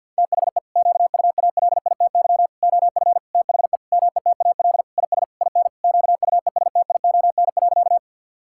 Morse code, heard as a keyed tone, about 50 wpm, 700 Hz. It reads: THE QUALITY OF THE METAL IS IN QUESTION.
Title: Sentences from Top 1000 Words - Review Part 6 - 50wpm